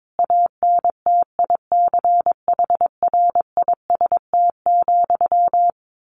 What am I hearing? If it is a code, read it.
Morse code, heard as a keyed tone, about 22 wpm, 700 Hz. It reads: ANTICHRIST,
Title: A Sound Of Thunder 22wpm